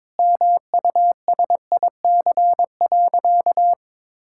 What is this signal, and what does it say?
Morse code, heard as a keyed tone, about 21 wpm, 700 Hz. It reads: MUSIC.
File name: practice / 22wpm / Sentences with 3-Word Phrases - 22wpm